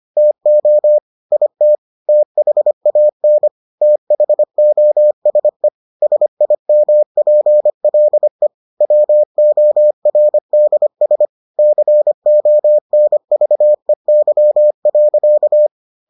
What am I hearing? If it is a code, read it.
Morse code, heard as a keyed tone, about 25 wpm, 600 Hz. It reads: TO IT THAN THOSE SIMPLE WORDS CONVEY.